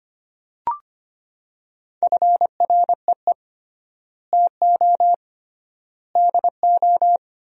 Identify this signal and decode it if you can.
Morse code, heard as a keyed tone, about 25 wpm, 700 Hz. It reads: FREE TO DO
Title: 3x Word-Spacing - Sentences from Top 500 Words - 25wpm